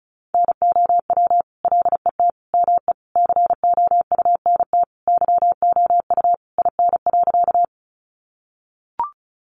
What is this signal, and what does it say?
Morse code, heard as a keyed tone, about 34 wpm, 700 Hz. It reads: NOW LET ME COUNT YOU IN.
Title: Sentences from Top 1500 Words 35wpm